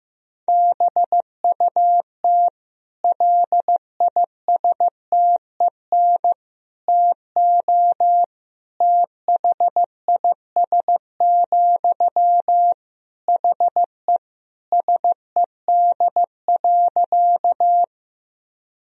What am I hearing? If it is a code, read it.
Morse code, heard as a keyed tone, about 15 wpm, 700 Hz. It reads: BUT LISTEN TO THIS, HE SED.
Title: Sentences from Top 400 Words and Abbreviations 15wpm